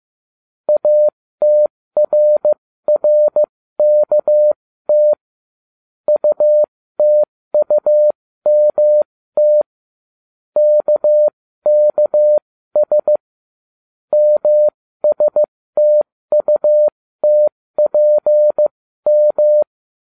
Morse code, heard as a keyed tone, about 15 wpm, spaced as for 13 wpm, 600 Hz. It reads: ATRRKT UTUMT KKS MSTUTPM